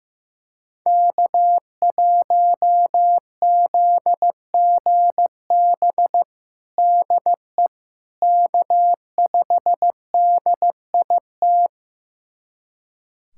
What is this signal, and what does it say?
Morse code, heard as a keyed tone, about 15 wpm, 700 Hz. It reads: K1ZGB DE K5DIT
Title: USA Callsign DE Callsign - 5 Characters - 15wpm